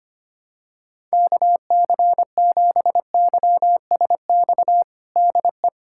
Morse code, heard as a keyed tone, about 25 wpm, 700 Hz. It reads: KC7YSX DE
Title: USA Callsign DE Callsign - 6 Characters - 25wpm